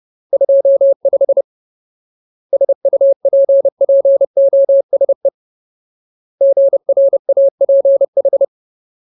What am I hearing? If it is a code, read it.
Morse code, heard as a keyed tone, about 30 wpm, 550 Hz. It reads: 25 SUPPOSE GRAPH
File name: Ultimate Word List / 30WPM Ultimate